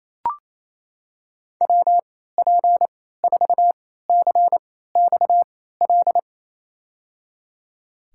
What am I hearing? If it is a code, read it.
Morse code, heard as a keyed tone, about 28 wpm, spaced as for 16 wpm, 700 Hz. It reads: WP4CXL
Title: USA Call Signs - 6 Characters - 3x Character Spacing - 28wpm